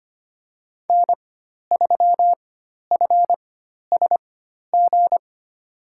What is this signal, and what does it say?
Morse code, heard as a keyed tone, about 25 wpm, spaced as for 12 wpm, 700 Hz. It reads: N3FSG
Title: USA Call Signs - 5 Characters - 4x Character Spacing - 25wpm